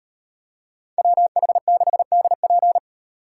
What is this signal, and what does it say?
Morse code, heard as a keyed tone, about 35 wpm, 700 Hz. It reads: WH6DP